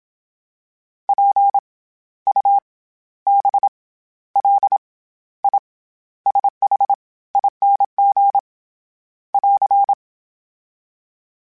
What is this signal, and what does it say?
Morse code, heard as a keyed tone, about 27 wpm, 800 Hz. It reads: P U B L I SH ING <AR>
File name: Special INGs-mp3s / Special INGs-mp3s / SlowToFaster ing @25